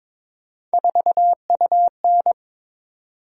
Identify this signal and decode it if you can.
Morse code, heard as a keyed tone, about 22 wpm, 700 Hz. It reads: 4UN
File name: practice / 22wpm / Three Letter-Number Combinations - 22wpm